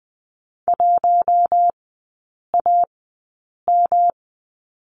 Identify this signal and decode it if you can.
Morse code, heard as a keyed tone, about 20 wpm, spaced as for 8 wpm, 700 Hz. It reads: 1AM
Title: 107 Lesson 18e - Letter m - Three Characters 20wpm